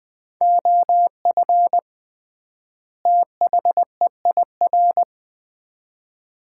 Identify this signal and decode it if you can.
Morse code, heard as a keyed tone, about 20 wpm, 700 Hz. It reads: OF THEIR
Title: 3x Word-Spacing - Sets of 2 Words - 20wpm